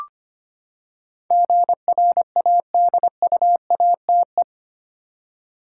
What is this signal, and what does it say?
Morse code, heard as a keyed tone, about 25 wpm, 700 Hz. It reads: GRADUATE
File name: Eight Letter Words - 25wpm